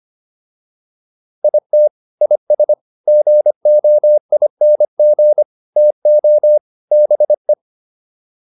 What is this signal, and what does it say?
Morse code, heard as a keyed tone, about 25 wpm, 600 Hz. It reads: IT IS GOING TO BE